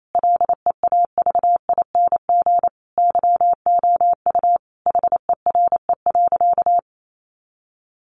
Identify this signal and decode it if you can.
Morse code, heard as a keyed tone, about 28 wpm, 700 Hz. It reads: LEAVING YOU HERE.
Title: A Sound Of Thunder 28wpm